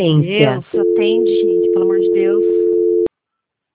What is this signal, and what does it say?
Dial tone (North America), 350 + 440 Hz, with speech.
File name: out- -2031- - - .469